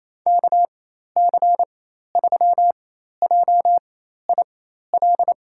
Morse code, heard as a keyed tone, about 28 wpm, spaced as for 13 wpm, 700 Hz. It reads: KC3JIL
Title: USA Call Signs - 4x Character Spacing - 28wpm